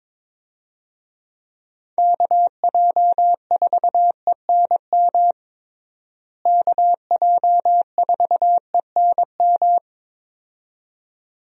Morse code, heard as a keyed tone, about 22 wpm, 700 Hz. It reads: KJ4ENM KJ4ENM
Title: USA Call Signs - 6 Characters - Repeated 2x - 22wpm